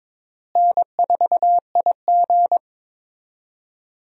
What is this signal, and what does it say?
Morse code, heard as a keyed tone, about 22 wpm, 700 Hz. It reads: N4IG
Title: USA Call Signs - 4 Characters - Repeated 2x - 22wpm